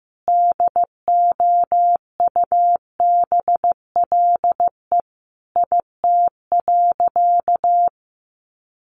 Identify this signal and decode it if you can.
Morse code, heard as a keyed tone, about 15 wpm, 700 Hz. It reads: DOUBLE IT.
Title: Sentences from Top 1400 Words 15wpm